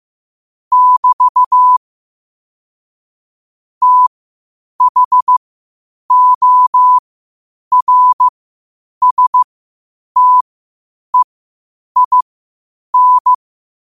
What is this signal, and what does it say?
Morse code, heard as a keyed tone, about 15 wpm, spaced as for 8 wpm, 1000 Hz. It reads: <BT> THORSTEIN
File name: qotdfiles / 8wpm